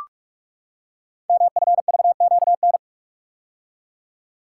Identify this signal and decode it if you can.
Morse code, heard as a keyed tone, about 44 wpm, 700 Hz. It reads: MFVQN
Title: Code Groups - 5 Letters - 45wpm